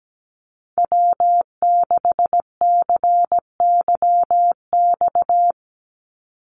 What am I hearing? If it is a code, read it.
Morse code, heard as a keyed tone, about 17 wpm, 700 Hz. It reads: W6CYX
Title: Call Signs Repeated 3x - 17wpm